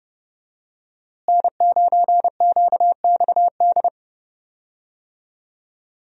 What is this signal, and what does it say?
Morse code, heard as a keyed tone, about 32 wpm, 700 Hz. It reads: N9QXD
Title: USA Call Signs - 5 Characters - Repeated 2x - 30wpm